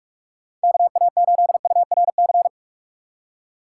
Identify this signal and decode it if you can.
Morse code, heard as a keyed tone, about 44 wpm, 700 Hz. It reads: KA8URC
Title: USA Call Signs - 6 Characters - 3x Character Spacing - 45wpm